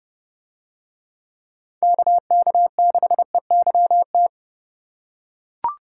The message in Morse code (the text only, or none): KK6EYT